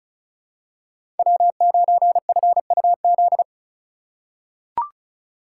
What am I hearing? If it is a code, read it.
Morse code, heard as a keyed tone, about 36 wpm, 700 Hz. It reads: W9FUZ
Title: USA Call Signs - 5 Characters - 3x Character Spacing - 35wpm